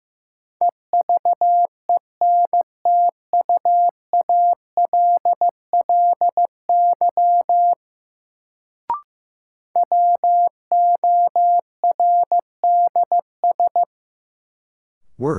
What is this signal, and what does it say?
Morse code, heard as a keyed tone, about 15 wpm, 700 Hz. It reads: EVENTUALLY WORDS
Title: Sentences from Top 1000 Words - Review Part 6 - New Word Drill - 15wpm